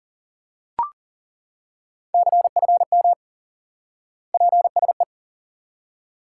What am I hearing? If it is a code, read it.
Morse code, heard as a keyed tone, about 40 wpm, 700 Hz. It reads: CFM PSE
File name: IQE Course 3 - Lesson 89 - YAGI - Two Elements - 40wpm